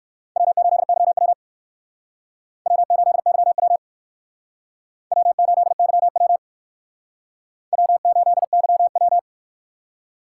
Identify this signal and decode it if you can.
Morse code, heard as a keyed tone, about 62 wpm, 700 Hz. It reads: W8YW W8YW W8YW W8YW